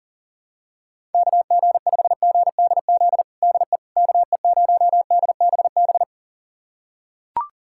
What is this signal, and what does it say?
Morse code, heard as a keyed tone, about 40 wpm, 700 Hz. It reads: KG5GDZ DE KE0DBB
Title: USA Callsign DE Callsign - 6 Characters - 40wpm